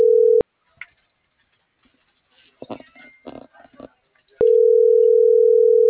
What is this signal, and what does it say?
Ringback tone (North America), 440 + 480 Hz.